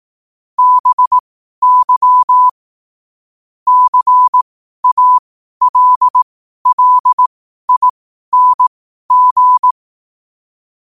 Morse code, heard as a keyed tone, about 18 wpm, spaced as for 12 wpm, 1000 Hz. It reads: BY CALLING
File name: qotdfiles / 12wpm